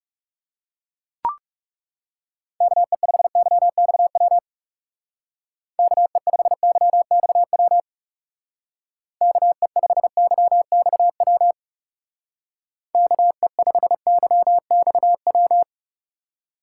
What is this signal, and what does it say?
Morse code, heard as a keyed tone, about 36 wpm, 700 Hz. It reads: KE5YXW KE5YXW KE5YXW KE5YXW
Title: Speed Racing - USA 2x3 Call Signs 30wpm